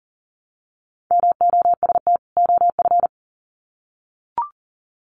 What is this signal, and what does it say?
Morse code, heard as a keyed tone, about 40 wpm, 700 Hz. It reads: MOST OF
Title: Sets of 2 Words 40wpm